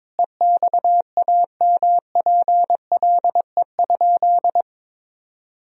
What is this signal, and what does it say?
Morse code, heard as a keyed tone, about 22 wpm, 700 Hz. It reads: EXAMPLE?
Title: Sentences from Top 1000 Words - Review Part 1 - 22wpm